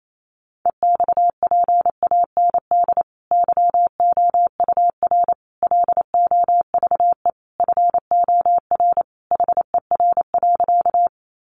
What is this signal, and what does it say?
Morse code, heard as a keyed tone, about 28 wpm, 700 Hz. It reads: EXPAND YOUR LOVE FOR HER.
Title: Sentences from Top 1400 Words 28wpm